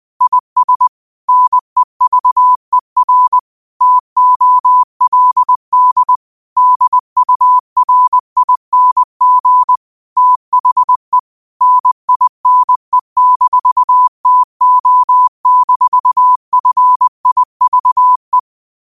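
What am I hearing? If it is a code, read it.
Morse code, heard as a keyed tone, about 20 wpm, 1000 Hz. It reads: IS NEVER TOLD DURING THE NINE-TO-FIVE
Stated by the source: Quote of the day sent in Morse at 20 words per minute.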